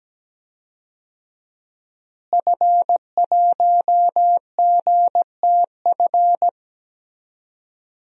Morse code, heard as a keyed tone, about 17 wpm, 700 Hz. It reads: F1GTF
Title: DXCC-100 Call Signs - 5 Characters - Repeated 2x - 17wpm